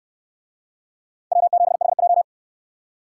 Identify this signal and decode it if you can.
Morse code, heard as a keyed tone, about 66 wpm, 700 Hz. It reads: W7HJ